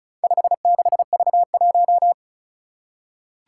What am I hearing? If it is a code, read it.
Morse code, heard as a keyed tone, about 36 wpm, 700 Hz. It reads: 56V1
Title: Four Challenging Letter-Number Combinations 35wpm